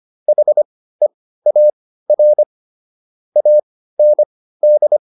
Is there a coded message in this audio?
Yes: Morse code from 0.3 to 5.0 s, heard as a keyed tone, about 25 wpm, spaced as for 15 wpm, 600 Hz. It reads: HEAR AND